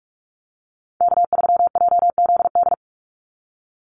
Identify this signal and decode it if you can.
Morse code, heard as a keyed tone, about 44 wpm, 700 Hz. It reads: K3JZD
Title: NA SOTA Chaser Callsigns 45wpm